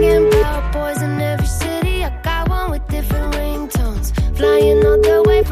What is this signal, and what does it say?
Ringback tone (Europe, 425 Hz), with noise.